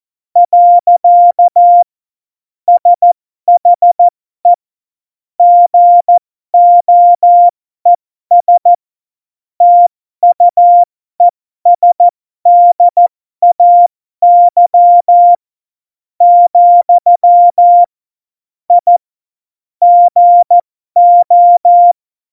Morse code, heard as a keyed tone, about 14 wpm, spaced as for 12 wpm, 700 Hz. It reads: . SHE GOES TUESDAY , I GO
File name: Morsekurs / Witze